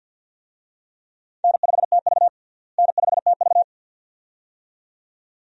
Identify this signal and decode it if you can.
Morse code, heard as a keyed tone, about 48 wpm, 700 Hz. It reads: N5TV N5TV
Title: USA Call Signs - 4 Characters - Repeated 2x - 50wpm